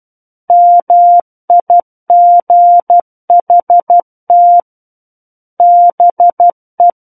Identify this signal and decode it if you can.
Morse code, heard as a keyed tone, about 12 wpm, 700 Hz. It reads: MIGHT BE
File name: Book One / 03-Chapter 3 - 12 WPM